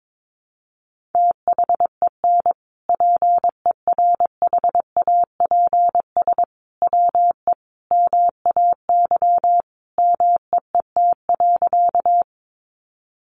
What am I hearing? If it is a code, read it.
Morse code, heard as a keyed tone, about 22 wpm, 700 Hz. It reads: THEN PERHAPS WE MAY MEET.
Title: Sentences from Top 1100 Words 22wpm